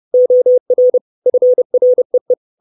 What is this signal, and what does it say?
Morse code, heard as a keyed tone, about 30 wpm, 500 Hz. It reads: OR FREE
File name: 632 Words 30WPM